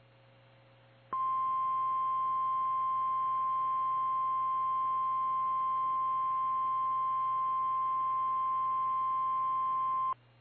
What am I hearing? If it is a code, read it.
Emergency Alert System (SAME) tones (the 1050 Hz attention tone). It is heard from 1.1 to 10.2 s.
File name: eas / KKPRKXPN / KKPRKXPN